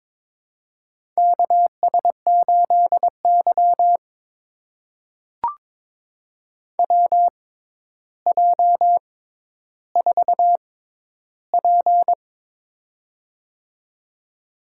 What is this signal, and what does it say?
Morse code, heard as a keyed tone, about 23 wpm, spaced as for 11 wpm, 700 Hz. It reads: KS8Y WJ4P
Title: USA Call Signs - 4 Characters - 6x Character Spacing - 22wpm